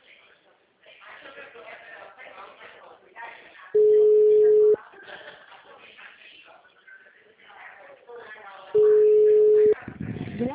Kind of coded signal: telephone call progress tone

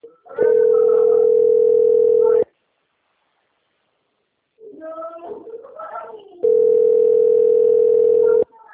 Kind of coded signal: telephone call progress tone

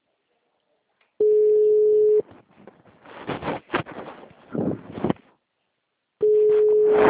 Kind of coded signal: telephone call progress tone

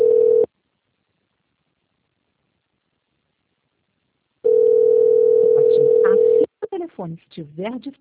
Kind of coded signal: telephone call progress tone

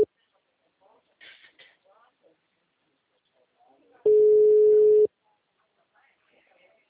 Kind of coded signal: telephone call progress tone

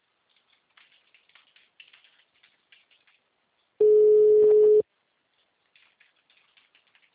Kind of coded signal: telephone call progress tone

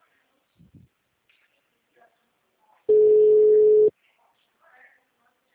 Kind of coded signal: telephone call progress tone

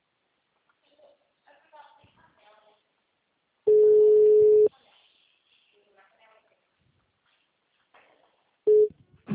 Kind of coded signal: telephone call progress tone

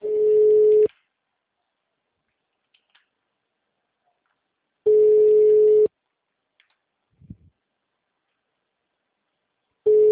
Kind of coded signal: telephone call progress tone

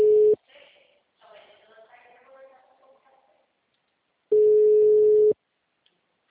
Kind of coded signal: telephone call progress tone

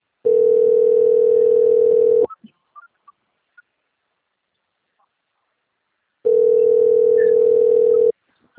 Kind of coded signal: telephone call progress tone

